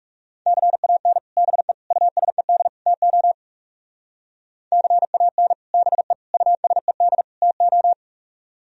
Morse code, heard as a keyed tone, about 37 wpm, 690 Hz. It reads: CAN BE USED TO CAN BE USED TO
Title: Speed-Racing - Sets of 4 Words 30wpm